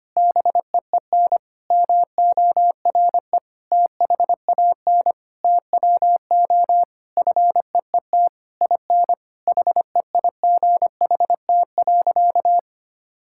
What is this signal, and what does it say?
Morse code, heard as a keyed tone, about 25 wpm, 700 Hz. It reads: BEEN MORE THAN TWO FEET IN HEIGHT.
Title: Sentences from Top 1600 Words 25wpm